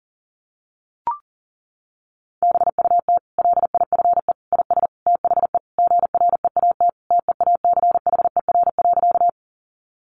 Morse code, heard as a keyed tone, about 40 wpm, 700 Hz. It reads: BUT LIFE IS THE GREAT TEACHER.
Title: Sentences from Top 900 Words 40wpm